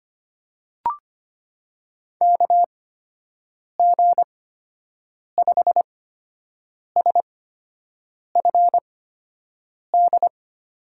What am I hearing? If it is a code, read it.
Morse code, heard as a keyed tone, about 25 wpm, spaced as for 7 wpm, 700 Hz. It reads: KG5SFD
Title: USA Call Signs - 6 Characters - 8x Character Spacing - 25wpm